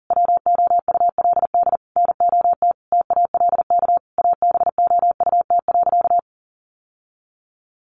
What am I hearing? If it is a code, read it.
Morse code, heard as a keyed tone, about 40 wpm, 700 Hz. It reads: WOULD NOT TALK ABOUT.
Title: Sentences from Top 800 Words 40wpm